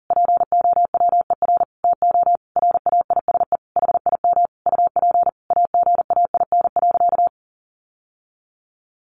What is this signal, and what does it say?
Morse code, heard as a keyed tone, about 37 wpm, 700 Hz. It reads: POWER TO RAISE HIM UP AGAIN.
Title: Sentences from Top 1400 Words 40wpm